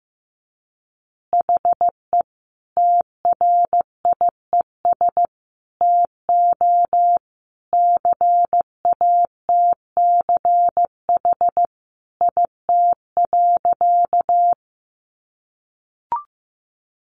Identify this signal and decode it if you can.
Morse code, heard as a keyed tone, about 15 wpm, 700 Hz. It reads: HE TRIES TO CATCH IT.
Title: Additional Easy Phrases 15wpm